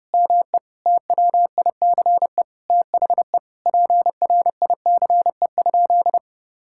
Morse code, heard as a keyed tone, about 30 wpm, 700 Hz. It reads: ME TWICE THE PRICE?
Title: Sentences from Top 1600 Words 30wpm